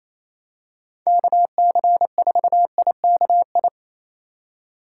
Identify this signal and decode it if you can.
Morse code, heard as a keyed tone, about 28 wpm, 700 Hz. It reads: KC4IKI
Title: USA Call Signs - 2x Character Spacing - 28wpm